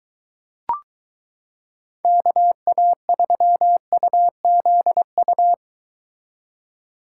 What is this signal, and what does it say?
Morse code, heard as a keyed tone, about 23 wpm, 700 Hz. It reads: KA3UZU